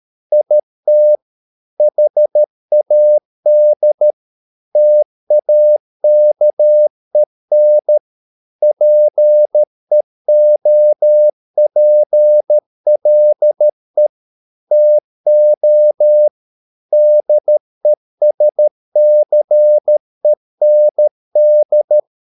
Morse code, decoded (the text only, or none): IT HAD TAKEN PEOPLE TO DESCEND